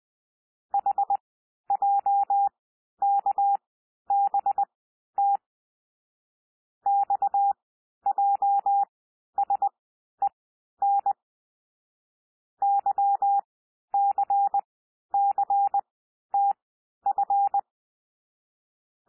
Morse code, heard as a keyed tone, about 20 wpm, spaced as for 11 wpm, 800 Hz. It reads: HJKBT XJSEN YCCTF
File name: cwmp3 / L3 SOUND LETTERS 20 wpm